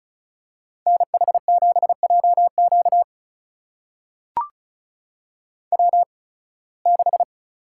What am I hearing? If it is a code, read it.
Morse code, heard as a keyed tone, about 36 wpm, spaced as for 23 wpm, 700 Hz. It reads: NH7JQ W6
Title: USA Call Signs - 5 Characters - 8x Character Spacing - 35wpm